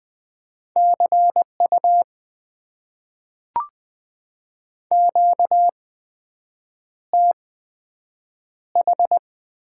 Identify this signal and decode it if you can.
Morse code, heard as a keyed tone, about 20 wpm, spaced as for 7 wpm, 700 Hz. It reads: CU QTH
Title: Top 90 QSO Elements - 8x Character Spacing - 20wpm